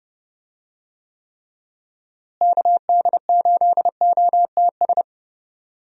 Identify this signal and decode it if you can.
Morse code, heard as a keyed tone, about 30 wpm, 700 Hz. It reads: KD8OTS
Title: USA Call Signs - 6 Characters - Repeated 3x - 30wpm